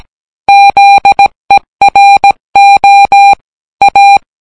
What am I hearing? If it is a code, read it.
Morse code, heard as a keyed tone, about 17 wpm, 800 Hz. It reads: ZERO A